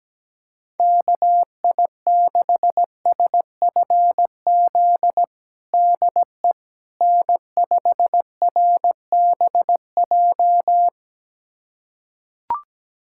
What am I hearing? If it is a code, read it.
Morse code, heard as a keyed tone, about 17 wpm, 700 Hz. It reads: KI6SFZ DE N5RBJ
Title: USA Callsign DE Callsign 17wpm